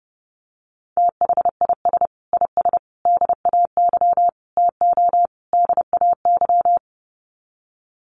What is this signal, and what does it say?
Morse code, heard as a keyed tone, about 30 wpm, 700 Hz. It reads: THIS IS DAY TO DAY
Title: Sets of 5 Words Encore 30wpm